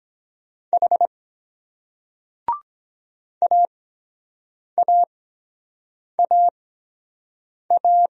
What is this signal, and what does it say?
Morse code, heard as a keyed tone, about 23 wpm, spaced as for 7 wpm, 700 Hz. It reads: H AAAA